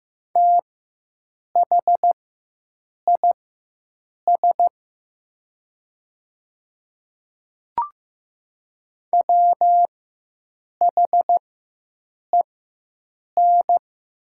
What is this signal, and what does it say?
Morse code, heard as a keyed tone, about 15 wpm, spaced as for 7 wpm, 700 Hz. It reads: THIS WHEN
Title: 4x Character Spacing - Top 100 Words - 15wpm